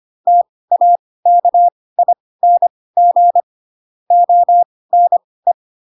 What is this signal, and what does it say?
Morse code, heard as a keyed tone, about 25 wpm, spaced as for 18 wpm, 700 Hz. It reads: TAKING ONE
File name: Pinocchio / Pinocchio ch4 12 25 ik1hge